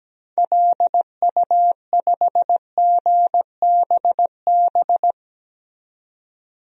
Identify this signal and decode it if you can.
Morse code, heard as a keyed tone, about 17 wpm, 700 Hz. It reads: LU5GBB
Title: DXCC-100 Call Signs - 6 Characters - 17wpm